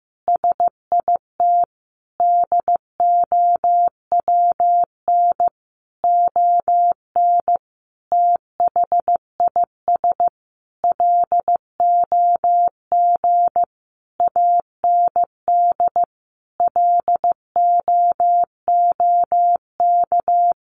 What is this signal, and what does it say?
Morse code, heard as a keyed tone, about 15 wpm, 700 Hz. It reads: SIT DOWN ON THIS LOG AND LOOK
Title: Sentences from Top 1400 Words 15wpm